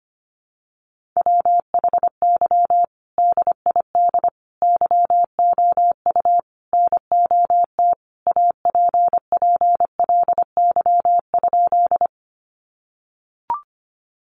Morse code, heard as a keyed tone, about 25 wpm, 700 Hz. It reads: WHY DID YOU NOT APPLY?
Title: Sentences from Top 1200 Words 25wpm